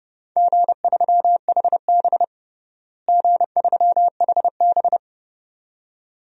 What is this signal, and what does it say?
Morse code, heard as a keyed tone, about 30 wpm, 700 Hz. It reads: G3HB G3HB
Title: DXCC-100 Call Signs - 4 Characters - Repeated 2x - 30wpm